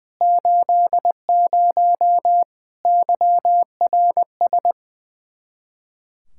Morse code, heard as a keyed tone, about 21 wpm, 700 Hz. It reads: 80 YRS
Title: K1USN SST QSO Practice 20wpm